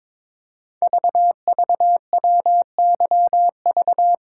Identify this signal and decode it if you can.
Morse code, heard as a keyed tone, about 22 wpm, 700 Hz. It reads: VVWYV
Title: Code Groups - 5 Letters - 22wpm